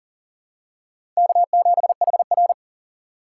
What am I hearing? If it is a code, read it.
Morse code, heard as a keyed tone, about 40 wpm, 700 Hz. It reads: K7HR
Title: USA Call Signs - 4 Characters - 2x Character Spacing - 40wpm